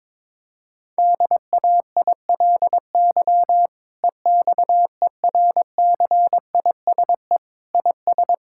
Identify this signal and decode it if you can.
Morse code, heard as a keyed tone, about 22 wpm, 700 Hz. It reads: DAILY EXERCISE IS
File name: Difficult Sentences with Hints - 22wpm